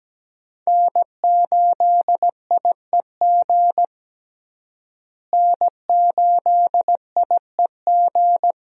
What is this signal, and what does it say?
Morse code, heard as a keyed tone, about 17 wpm, 700 Hz. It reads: N8IEG N8IEG
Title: USA Call Signs - 5 Characters - Repeated 2x - 17wpm